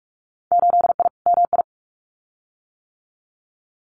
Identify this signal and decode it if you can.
Morse code, heard as a keyed tone, about 46 wpm, 700 Hz. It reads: 8I MI
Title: Winter Field Day Exchanges Encore 45wpm